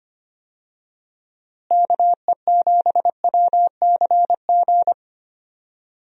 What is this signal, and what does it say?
Morse code, heard as a keyed tone, about 25 wpm, 700 Hz. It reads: KE7WCG
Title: USA Call Signs - 6 Characters - Repeated 3x - 25wpm